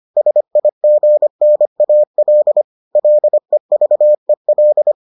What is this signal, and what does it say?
Morse code, heard as a keyed tone, about 25 wpm, 600 Hz. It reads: SIGNAL LEVEL